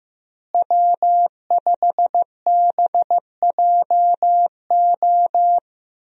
Morse code, heard as a keyed tone, about 15 wpm, 700 Hz. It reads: W5BJO
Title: USA Call Signs - 5 Characters - Repeated 3x - 15wpm